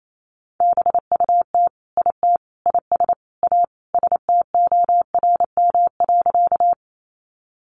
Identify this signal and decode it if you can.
Morse code, heard as a keyed tone, about 28 wpm, 700 Hz. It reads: BUT IT IS A STORM.
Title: Sentences from Top 1500 Words 28wpm